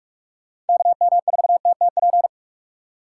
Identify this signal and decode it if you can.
Morse code, heard as a keyed tone, about 58 wpm, 700 Hz. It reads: KM4TTP